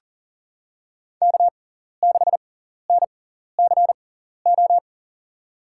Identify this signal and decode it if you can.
Morse code, heard as a keyed tone, about 38 wpm, spaced as for 14 wpm, 700 Hz. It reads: K6NCO
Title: USA Call Signs - 5 Characters - 6x Character Spacing - 40wpm